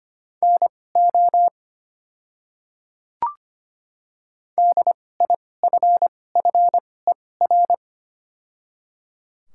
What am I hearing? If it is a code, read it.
Morse code, heard as a keyed tone, about 25 wpm, spaced as for 18 wpm, 700 Hz. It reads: NO DIFFER
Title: 2x Character Spacing - Top 200 Words - 25wpm